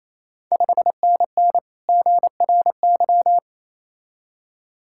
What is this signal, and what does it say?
Morse code, heard as a keyed tone, about 27 wpm, 690 Hz. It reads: 5NN GRY
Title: Ontario QSO Party Exchange 28wpm